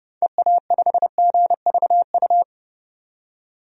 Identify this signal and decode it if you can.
Morse code, heard as a keyed tone, about 30 wpm, 700 Hz. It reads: EA5GVU
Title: DXCC-100 Call Signs - 6 Characters - 30wpm